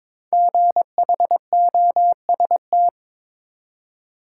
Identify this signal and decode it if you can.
Morse code, heard as a keyed tone, about 22 wpm, 700 Hz. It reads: GHOST
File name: 5 Letter Words - First 3 Letters Given - 22wpm